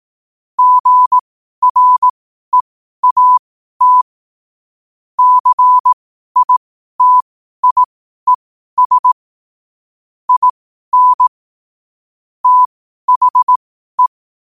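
Morse code, heard as a keyed tone, about 18 wpm, spaced as for 12 wpm, 1000 Hz. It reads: GREAT CITIES IN THE
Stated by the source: 12 wpm Morse.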